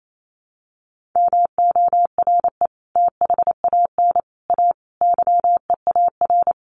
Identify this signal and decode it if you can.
Morse code, heard as a keyed tone, about 28 wpm, 700 Hz. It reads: MORE THAN A YEAR
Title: Sets of 5 Words Encore2 28wpm